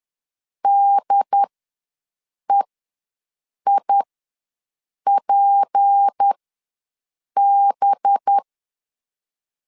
Morse code, heard as a keyed tone, about 10 wpm, spaced as for 6 wpm, 800 Hz. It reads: DEIPB
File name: LEZIONI / 6 PKX0 / 6 P ws 10wpm 3751